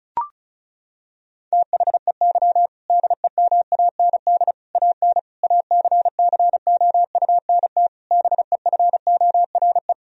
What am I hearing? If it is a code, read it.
Morse code, heard as a keyed tone, about 34 wpm, 700 Hz. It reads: THEY DEMAND AN ACCOUNT BEFORE
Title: Sentences from Top 1000 Words - Review Part 4 - 35wpm